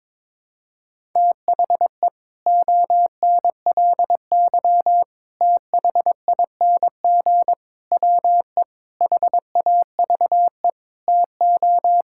Morse code, heard as a keyed tone, about 22 wpm, 700 Hz. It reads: THE ONLY THING WE HAVE TO
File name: practice / 22wpm / Sets of 8 Words - 22wpm